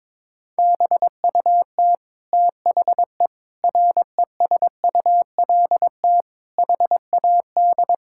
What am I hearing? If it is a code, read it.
Morse code, heard as a keyed tone, about 22 wpm, 700 Hz. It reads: BUT THE RESULT HAD
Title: Sentences from Top 1000 Words - Review Part 5 - 22wpm